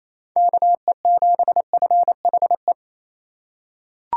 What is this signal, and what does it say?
Morse code, heard as a keyed tone, about 28 wpm, 700 Hz. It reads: KE7FHE
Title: USA Call Signs - 6 Characters - 3x Character Spacing - 28wpm